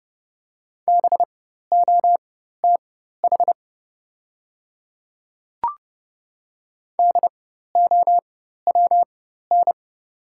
Morse code, heard as a keyed tone, about 30 wpm, spaced as for 14 wpm, 700 Hz. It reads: BOTH DOWN
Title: 4x Character Spacing - Top 300 Words - 30wpm